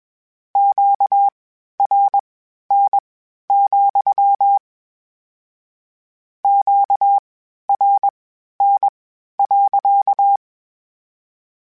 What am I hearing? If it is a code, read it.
Morse code, heard as a keyed tone, about 21 wpm, spaced as for 12 wpm, 800 Hz. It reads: QRN, QRN.
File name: Q Calls / Q Calls-mp3s / Q Calls sppc @20